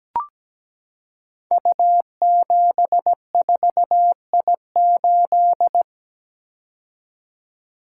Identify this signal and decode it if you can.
Morse code, heard as a keyed tone, about 17 wpm, 700 Hz. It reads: U74I8
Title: Code Groups - 5 Letter:Numbers - 17wpm